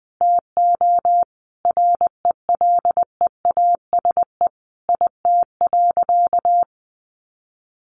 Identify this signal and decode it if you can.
Morse code, heard as a keyed tone, about 20 wpm, 700 Hz. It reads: TO RELEASE IT.
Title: Sentences from Top 1100 Words 20wpm